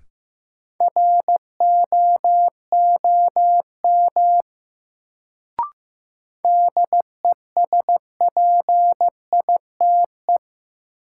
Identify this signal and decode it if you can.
Morse code, heard as a keyed tone, about 15 wpm, 700 Hz. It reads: ROOM DESPITE
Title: Sets of 4 Words - New Word Drill - 15wpm